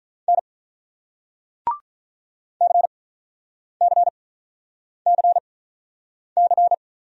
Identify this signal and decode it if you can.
Morse code, heard as a keyed tone, about 44 wpm, spaced as for 9 wpm, 700 Hz. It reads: N CCCC